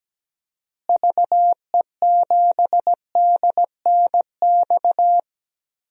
Morse code, heard as a keyed tone, about 17 wpm, 700 Hz. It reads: VE7DNX
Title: DXCC-100 Call Signs - 6 Characters - Repeated 2x - 17wpm